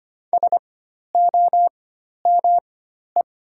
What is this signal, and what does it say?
Morse code, heard as a keyed tone, about 25 wpm, spaced as for 12 wpm, 700 Hz. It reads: SOME
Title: 4x Character Spacing - Top 300 Words - 25wpm